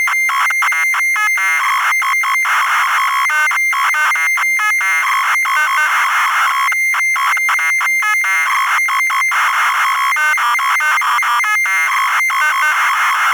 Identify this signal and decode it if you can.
Dial-up modem handshake.
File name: Modem Drumloop